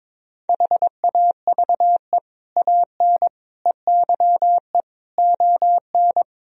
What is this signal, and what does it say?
Morse code, heard as a keyed tone, about 22 wpm, 700 Hz. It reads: HAVE AN EYE ON
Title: Sentences from Top 1000 Words - Review Part 1 - 22wpm